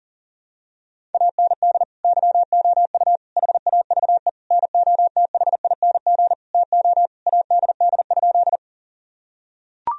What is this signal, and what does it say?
Morse code, heard as a keyed tone, about 40 wpm, 700 Hz. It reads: AND YOU HAVE NOTHING TO ADD?
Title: Sentences from Top 1000 Words - Review Part 1 - 40wpm